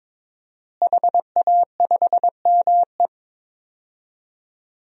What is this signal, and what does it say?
Morse code, heard as a keyed tone, about 22 wpm, 700 Hz. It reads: HA5ME
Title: DXCC-100 Call Signs - 5 Characters - Repeated 3x - 22wpm